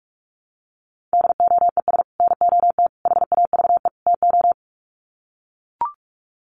Morse code, heard as a keyed tone, about 44 wpm, 700 Hz. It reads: DOES NOT HAVE TO
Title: Sets of 4 Words 45wpm